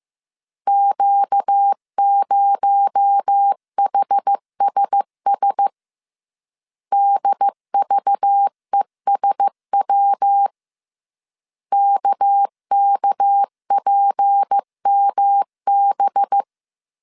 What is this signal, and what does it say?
Morse code, heard as a keyed tone, about 15 wpm, 800 Hz. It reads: Q0HSS DVESW KKPMB
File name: CW / DET / 1 6 1500C S B 14wpm 2958